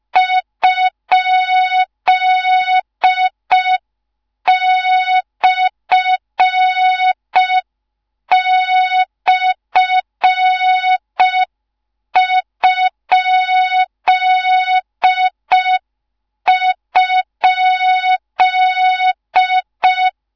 Morse code, heard as a keyed tone, about 5 wpm, 3700 Hz. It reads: ?//??